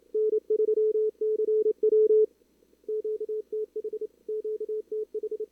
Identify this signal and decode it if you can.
Morse code, heard as a keyed tone, about 31 wpm, 440 Hz. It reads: N3CW QTH QTH
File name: CWSprint / Feb2022 / k5tr-feb-na-sprint-cw-k5ot -0000